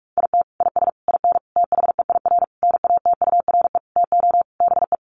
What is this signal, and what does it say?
Morse code, heard as a keyed tone, about 44 wpm, 700 Hz. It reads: IT IS IN THEIR NATURE TO BE
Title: Sentences from Top 1300 Words 45wpm